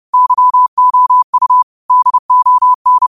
Morse code, heard as a keyed tone, about 30 wpm, 1000 Hz. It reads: YOU DON
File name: qotdfiles / 30wpm